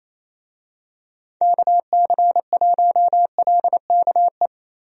Morse code, heard as a keyed tone, about 27 wpm, 700 Hz. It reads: KC1LKE
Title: USA Call Signs - 6 Characters - Repeated 3x - 28wpm